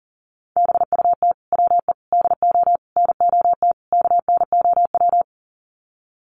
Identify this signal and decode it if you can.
Morse code, heard as a keyed tone, about 40 wpm, 700 Hz. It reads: BUT WE DO NOT KNOW
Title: Sets of 5 Words Encore2 40wpm